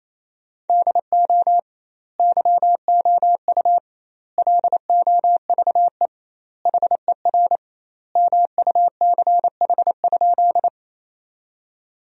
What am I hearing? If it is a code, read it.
Morse code, heard as a keyed tone, about 28 wpm, 700 Hz. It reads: DO YOU LOVE HER MUCH?
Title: 2x Word-Spacing - Sentences from Top 400 Words - 28wpm